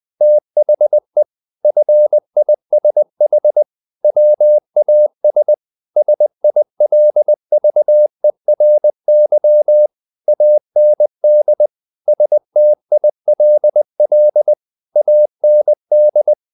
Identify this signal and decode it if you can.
Morse code, heard as a keyed tone, about 20 wpm, 600 Hz. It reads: THE FISH WAS SILVERY AND STILL AND
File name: the old man and the sea part 2 20wpm